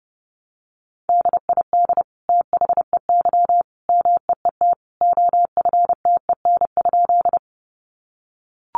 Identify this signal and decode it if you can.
Morse code, heard as a keyed tone, about 30 wpm, 700 Hz. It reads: DID THEY MEET OFTEN?
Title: Sentences from Top 1100 Words 30wpm